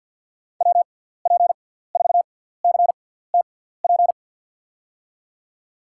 Morse code, heard as a keyed tone, about 53 wpm, spaced as for 17 wpm, 700 Hz. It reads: WP4CTP
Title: USA Call Signs - 6 Characters - 6x Character Spacing - 50wpm